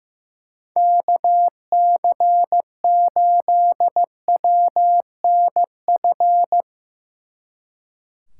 Morse code, heard as a keyed tone, about 15 wpm, 700 Hz. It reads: KC8WNF
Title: Speed Racing - USA 2x3 Call Signs 15wpm